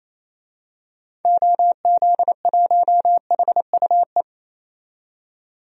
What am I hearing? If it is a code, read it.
Morse code, heard as a keyed tone, about 28 wpm, 700 Hz. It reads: OZ1HUE
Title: DXCC-100 Call Signs - 6 Characters - 28wpm